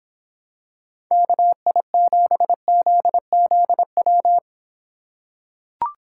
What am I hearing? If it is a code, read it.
Morse code, heard as a keyed tone, about 25 wpm, 700 Hz. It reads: KI7ZZW